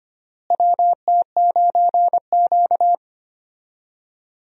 Morse code, heard as a keyed tone, about 25 wpm, 700 Hz. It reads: WT9Q
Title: USA Call Signs - 4 Characters - Repeated 3x - 25wpm